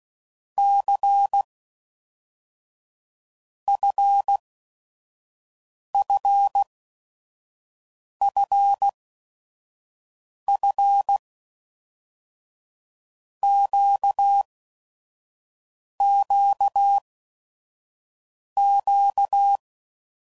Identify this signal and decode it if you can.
Morse code, heard as a keyed tone, about 16 wpm, 800 Hz. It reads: C F F F F Q Q Q
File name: Basic Letter Groups By7Groups / LFCYQ / LFCYQ@5 f15